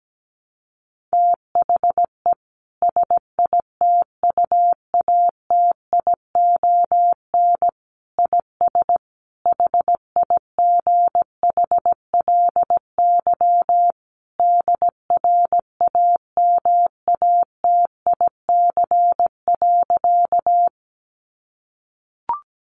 Morse code, heard as a keyed tone, about 17 wpm, 700 Hz. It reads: THE SITUATION IS HIGHLY DRAMATIC.
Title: Sentences from Top 1300 Words 17wpm